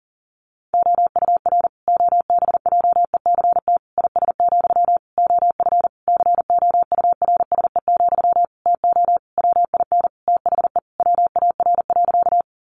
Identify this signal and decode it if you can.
Morse code, heard as a keyed tone, about 40 wpm, 700 Hz. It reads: OUR OBJECT IS, OF COURSE, TO WIN THE WAR.
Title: Sentences from Top 1500 Words 40wpm